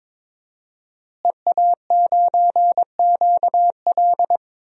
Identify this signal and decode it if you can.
Morse code, heard as a keyed tone, about 21 wpm, 700 Hz. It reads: EA9QL
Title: DXCC-100 Call Signs - 5 Characters - Repeated 3x - 22wpm